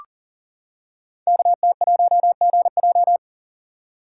Morse code, heard as a keyed tone, about 43 wpm, 690 Hz. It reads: KT1GJ
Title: Code Groups - 5 Letter:Numbers - 40wpm